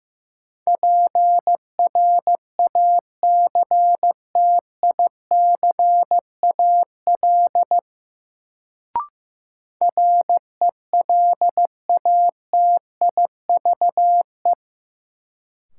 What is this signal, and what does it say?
Morse code, heard as a keyed tone, about 15 wpm, 700 Hz. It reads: PRACTICAL RELATIVE
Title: Sentences from Top 1400 Words - New Word Drill - 15wpm